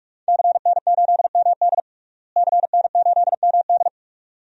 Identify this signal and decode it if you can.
Morse code, heard as a keyed tone, about 46 wpm, 700 Hz. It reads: CN8MD CN8MD
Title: DXCC-100 Call Signs - 5 Characters - Repeated 2x - 45wpm